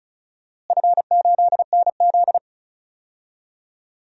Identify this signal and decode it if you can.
Morse code, heard as a keyed tone, about 34 wpm, 700 Hz. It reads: F8NZ
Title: DXCC-100 Call Signs - 4 Characters - Repeated 2x - 35wpm